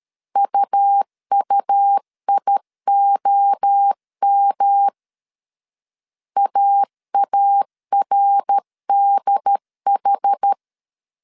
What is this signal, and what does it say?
Morse code, heard as a keyed tone, about 13 wpm, 800 Hz. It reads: UUIOM AARDH
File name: CW / DET / 1 3 1500C S A 12wpm 2905